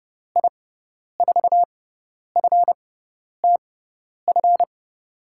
Morse code, heard as a keyed tone, about 30 wpm, spaced as for 10 wpm, 700 Hz. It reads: I4FTF
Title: USA Call Signs - 6 Characters - 6x Character Spacing - 30wpm